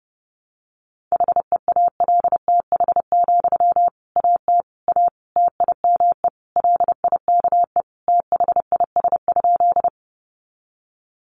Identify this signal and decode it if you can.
Morse code, heard as a keyed tone, about 30 wpm, 700 Hz. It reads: HEALTH, AT A TIME LIKE THIS?
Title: Sentences from Top 600 Words 30wpm